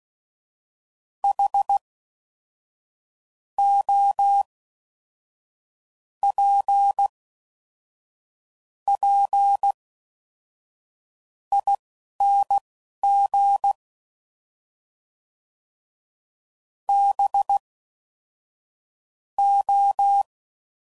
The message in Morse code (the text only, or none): HOPPING BO